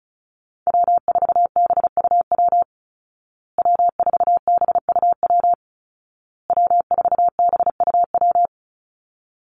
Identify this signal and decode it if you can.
Morse code, heard as a keyed tone, about 34 wpm, 700 Hz. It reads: W4BUW W4BUW W4BUW
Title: Call Signs Repeated 2x - 35wpm